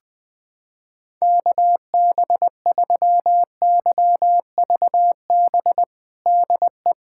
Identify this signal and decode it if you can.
Morse code, heard as a keyed tone, about 20 wpm, 700 Hz. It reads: KB3YVB DE
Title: USA Callsign DE Callsign - 6 Characters - 20wpm